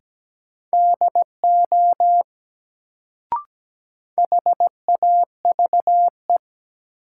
Morse code, heard as a keyed tone, about 17 wpm, 700 Hz. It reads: DO HAVE
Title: Sentences from Top 100 Words - New Word Drill - 17wpm